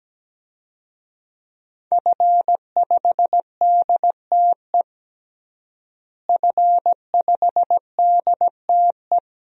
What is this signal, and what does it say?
Morse code, heard as a keyed tone, about 17 wpm, 700 Hz. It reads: F5DTE F5DTE
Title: DXCC-100 Call Signs - 5 Characters - Repeated 2x - 17wpm